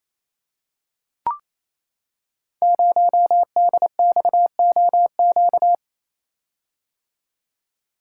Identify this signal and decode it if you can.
Morse code, heard as a keyed tone, about 28 wpm, 690 Hz. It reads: 0DXOQ
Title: Code Groups - 5 Letter:Numbers - 28wpm